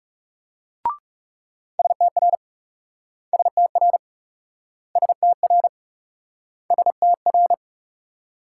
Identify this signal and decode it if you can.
Morse code, heard as a keyed tone, about 42 wpm, 700 Hz. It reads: STR STR STR STR